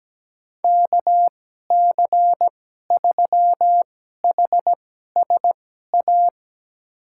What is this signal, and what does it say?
Morse code, heard as a keyed tone, about 17 wpm, spaced as for 12 wpm, 700 Hz. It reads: KC3HSA
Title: USA Call Signs - 6 Characters - 2x Character Spacing - 17wpm